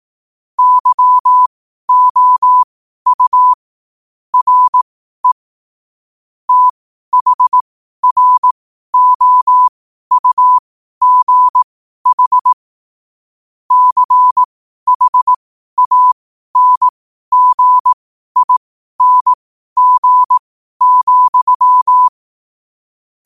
Morse code, heard as a keyed tone, about 18 wpm, spaced as for 12 wpm, 1000 Hz. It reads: YOU RE THROUGH CHANGING,
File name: qotdfiles / 12wpm